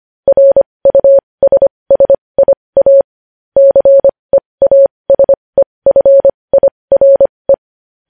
Morse code, heard as a keyed tone, about 25 wpm, spaced as for 20 wpm, 550 Hz. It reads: RUSSIA CEASEFIRE
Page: news-headlines-in-morse-code-at-20-wpm